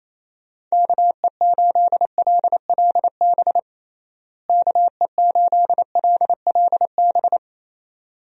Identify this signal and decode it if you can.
Morse code, heard as a keyed tone, about 28 wpm, 700 Hz. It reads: KE8LLB KE8LLB
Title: USA Call Signs - 6 Characters - Repeated 2x - 28wpm